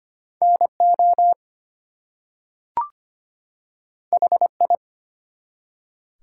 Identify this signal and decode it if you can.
Morse code, heard as a keyed tone, about 25 wpm, 700 Hz. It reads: NO HI
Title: Two Letter Words - 25wpm